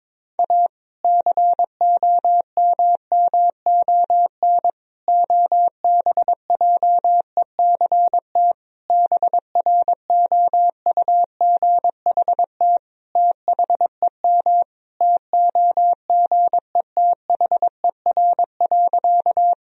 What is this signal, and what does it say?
Morse code, heard as a keyed tone, about 22 wpm, 700 Hz. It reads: A COMMON OBJECT BROUGHT THEM TOGETHER.
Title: Sentences from Top 1000 Words - Review Part 3 - 22wpm